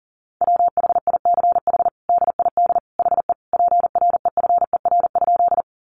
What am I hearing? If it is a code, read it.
Morse code, heard as a keyed tone, about 40 wpm, 700 Hz. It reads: WHICH DID HE PREFER?
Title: Sentences from Top 1100 Words 40wpm